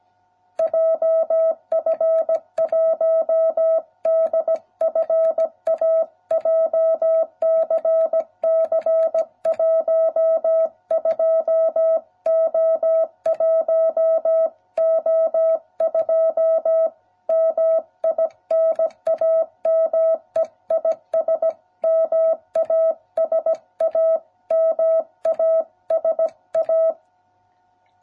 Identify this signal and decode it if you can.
Morse code, heard as a keyed tone, about 17 wpm, 660 Hz. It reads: JF1DFA JCC12O1O2 MI NAME IS MASA MASA